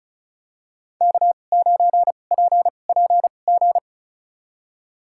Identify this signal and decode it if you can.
Morse code, heard as a keyed tone, about 36 wpm, spaced as for 25 wpm, 700 Hz. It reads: K9PPG
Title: USA Call Signs - 5 Characters - 2x Character Spacing - 35wpm